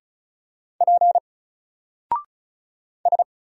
Morse code, heard as a keyed tone, about 36 wpm, 700 Hz. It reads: P S
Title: Single Character (Letter:Number:Punctuation:Prosign) 35wpm